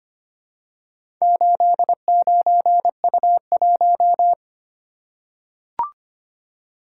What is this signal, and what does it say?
Morse code, heard as a keyed tone, about 25 wpm, 700 Hz. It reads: 89U1
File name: practice / 25wpm / Four Letter-Number Combinations - 25wpm